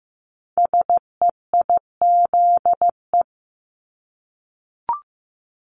Morse code, heard as a keyed tone, about 15 wpm, 700 Hz. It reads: SEIZE
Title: Multi-letter Phonemes Repeated 3x 15wpm